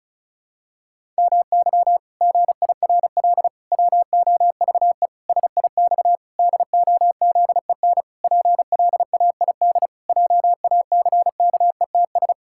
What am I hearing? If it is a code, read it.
Morse code, heard as a keyed tone, about 34 wpm, 700 Hz. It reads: MY GIRL WOVE SIX DOZEN PLAID JACKETS
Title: Pangram Challenge v2 - 35wpm